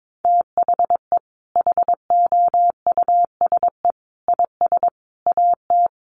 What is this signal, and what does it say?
Morse code, heard as a keyed tone, about 22 wpm, 700 Hz. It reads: THE HOUSE IS AT
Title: Sentences from Top 1500 Words 22wpm